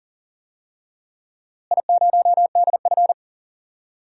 Morse code, heard as a keyed tone, about 40 wpm, 700 Hz. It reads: I0DF
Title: DXCC-100 Call Signs - 4 Characters - Repeated 3x - 40wpm